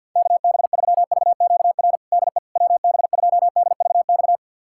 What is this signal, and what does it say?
Morse code, heard as a keyed tone, about 52 wpm, 700 Hz. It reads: KB3VQR DE WB2DVX
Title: USA Callsign DE Callsign - 6 Characters - 50wpm